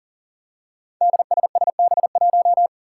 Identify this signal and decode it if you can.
Morse code, heard as a keyed tone, about 40 wpm, 700 Hz. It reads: DSSB1
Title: Five Challenging Letter-Number Combinations 40wpm